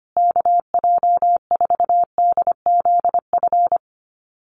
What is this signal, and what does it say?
Morse code, heard as a keyed tone, about 25 wpm, 700 Hz. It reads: KJ4DZF
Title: USA Call Signs Repeated 2x 25wpm